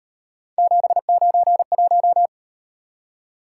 Morse code, heard as a keyed tone, about 37 wpm, 700 Hz. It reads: 791